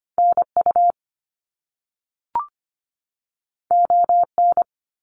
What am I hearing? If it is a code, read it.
Morse code, heard as a keyed tone, about 25 wpm, 700 Hz. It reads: NU ON
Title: Canadian Provinces and Territories 25wpm